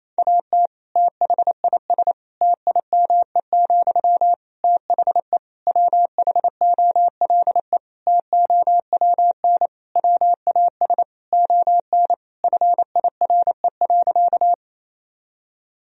Morse code, heard as a keyed tone, about 28 wpm, 700 Hz. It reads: AT THIS TIME, THE WHOLE TOWN WAS ON FIRE.
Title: Sentences from Top 1000 Words - Review Part 2 - 28wpm